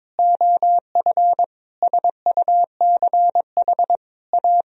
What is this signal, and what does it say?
Morse code, heard as a keyed tone, about 23 wpm, 700 Hz. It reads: OF SUCH A
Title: Sentences from Top 1000 Words - Review Part 3 - 22wpm